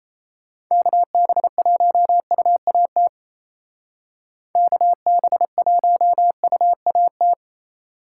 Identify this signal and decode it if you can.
Morse code, heard as a keyed tone, about 31 wpm, 700 Hz. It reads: KB1UAT KB1UAT
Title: Speed Racing - USA 2x3 Call Signs 28wpm